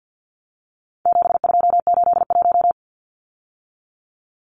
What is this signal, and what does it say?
Morse code, heard as a keyed tone, about 50 wpm, 700 Hz. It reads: 7281
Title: Numbers - 4-Digit 50wpm